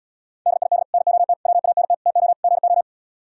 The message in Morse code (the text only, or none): DID NOT WANT TO GO